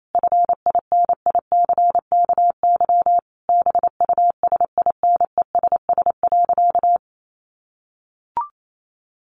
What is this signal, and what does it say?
Morse code, heard as a keyed tone, about 28 wpm, 700 Hz. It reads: FINICKY BUSINESS.
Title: A Sound Of Thunder 28wpm